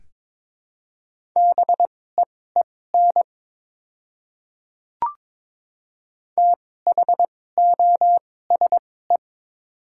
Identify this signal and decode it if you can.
Morse code, heard as a keyed tone, about 21 wpm, spaced as for 16 wpm, 700 Hz. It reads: BEEN THOSE
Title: 2x Character Spacing - Top 300 Words - 22wpm